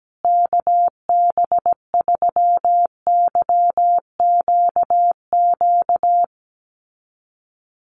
Morse code, heard as a keyed tone, about 17 wpm, 700 Hz. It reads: KB3YQQ
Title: USA Call Signs Repeated 3x 17wpm